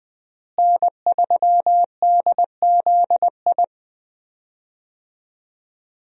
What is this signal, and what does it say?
Morse code, heard as a keyed tone, about 21 wpm, 700 Hz. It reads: N3DZI
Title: USA Call Signs - 5 Characters - Repeated 2x - 20wpm